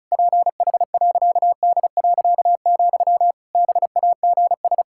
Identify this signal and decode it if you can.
Morse code, heard as a keyed tone, about 36 wpm, 700 Hz. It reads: PH.D., BAGS
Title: Pangram Challenge v2 - 35wpm